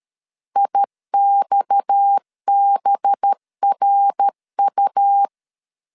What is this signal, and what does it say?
Morse code, heard as a keyed tone, about 13 wpm, 800 Hz. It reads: IXBRU
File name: LEZIONI / 6 PKX0 / 6 PKX0 12wpm 2231